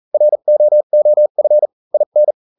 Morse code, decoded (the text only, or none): ROOF IN